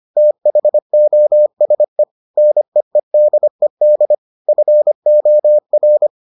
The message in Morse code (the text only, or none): THOSE NEEDED FOR